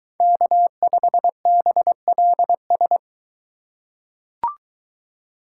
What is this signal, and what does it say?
Morse code, heard as a keyed tone, about 22 wpm, 700 Hz. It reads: K5BLS